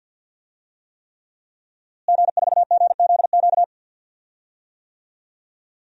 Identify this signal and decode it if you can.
Morse code, heard as a keyed tone, about 51 wpm, 700 Hz. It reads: G4GZQ
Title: DXCC-100 Call Signs - 5 Characters - Repeated 3x - 50wpm